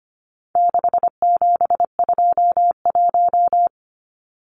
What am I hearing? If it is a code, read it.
Morse code, heard as a keyed tone, about 25 wpm, 700 Hz. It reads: 6721
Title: Numbers - 4-Digit 25wpm